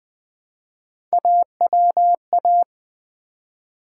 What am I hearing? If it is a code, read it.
Morse code, heard as a keyed tone, about 20 wpm, 700 Hz. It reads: AWA
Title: Build A Word - 1 Letter At A Time - 5-Letter Words - 20wpm